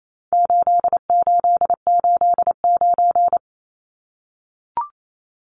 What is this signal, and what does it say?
Morse code, heard as a keyed tone, about 28 wpm, 700 Hz. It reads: 8889
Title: Numbers - 4-Digit 28wpm